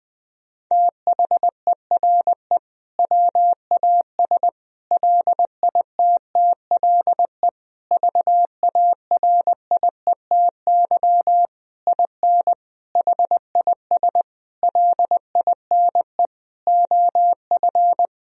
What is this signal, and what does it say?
Morse code, heard as a keyed tone, about 21 wpm, 700 Hz. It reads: THERE WAS LITTLE VARIETY IN HIS LINE OF
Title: Sentences from Top 1000 Words - Review Part 4 - 20wpm